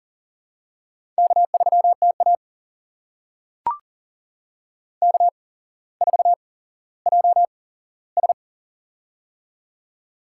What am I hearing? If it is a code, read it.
Morse code, heard as a keyed tone, about 41 wpm, spaced as for 17 wpm, 700 Hz. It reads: K3TA K4JS
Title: USA Call Signs - 4 Characters - 8x Character Spacing - 40wpm